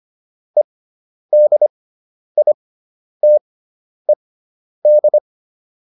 Morse code, heard as a keyed tone, about 25 wpm, spaced as for 10 wpm, 600 Hz. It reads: EDITED